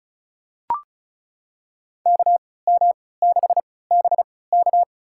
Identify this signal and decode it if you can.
Morse code, heard as a keyed tone, about 36 wpm, spaced as for 20 wpm, 700 Hz. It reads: KM6BK
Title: USA Call Signs - 5 Characters - 3x Character Spacing - 35wpm